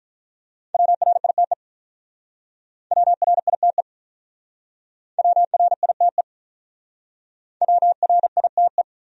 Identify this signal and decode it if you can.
Morse code, heard as a keyed tone, about 45 wpm, 700 Hz. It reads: WRITE WRITE WRITE WRITE